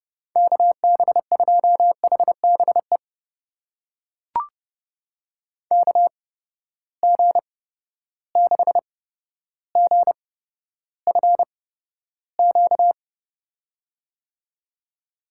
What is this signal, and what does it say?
Morse code, heard as a keyed tone, about 32 wpm, spaced as for 13 wpm, 700 Hz. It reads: KB2HBE KG6GFQ
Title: USA Call Signs - 6 Characters - 8x Character Spacing - 30wpm